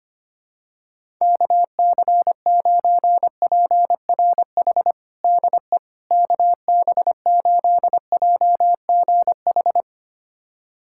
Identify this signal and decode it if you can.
Morse code, heard as a keyed tone, about 25 wpm, 700 Hz. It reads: KC9PRH DE KB8JGH
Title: USA Callsign DE Callsign - 6 Characters - 25wpm